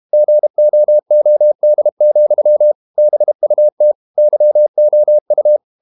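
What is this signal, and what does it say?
Morse code, heard as a keyed tone, about 32 wpm, 600 Hz. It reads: GOOD, BUT YOU
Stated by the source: Morse code practice at 32 wpm from CW med Gnister, reading Ken Follett's novel Never in English.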